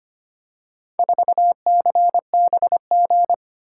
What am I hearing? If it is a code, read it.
Morse code, heard as a keyed tone, about 25 wpm, 700 Hz. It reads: 4CBG
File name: practice / 25wpm / Four Letter-Number Combinations - 25wpm